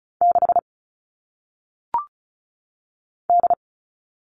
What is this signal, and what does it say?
Morse code, heard as a keyed tone, about 35 wpm, 700 Hz. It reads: 6 D
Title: All Cut Numbers - 1-Digit 35wpm